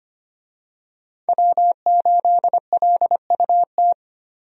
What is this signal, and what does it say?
Morse code, heard as a keyed tone, about 25 wpm, 700 Hz. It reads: W8LUT
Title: USA Call Signs - 5 Characters - Repeated 3x - 25wpm